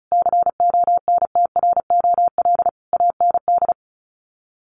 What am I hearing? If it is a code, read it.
Morse code, heard as a keyed tone, about 34 wpm, 700 Hz. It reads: CONTROL AND
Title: Sets of 5 Words Encore2 35wpm